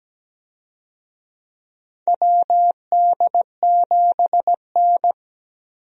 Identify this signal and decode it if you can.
Morse code, heard as a keyed tone, about 17 wpm, 700 Hz. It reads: WD7N
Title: USA Call Signs - 4 Characters - 17wpm